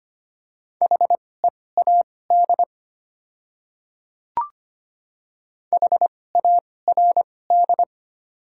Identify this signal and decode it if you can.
Morse code, heard as a keyed tone, about 25 wpm, spaced as for 18 wpm, 700 Hz. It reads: HEAD HARD
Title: 2x Character Spacing - Top 300 Words - 25wpm